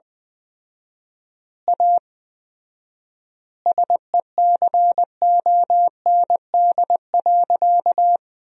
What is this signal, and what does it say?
Morse code, heard as a keyed tone, about 20 wpm, 700 Hz. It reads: A SECOND.
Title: 4x Word-Spacing - Sentences from Top 500 Words - 20wpm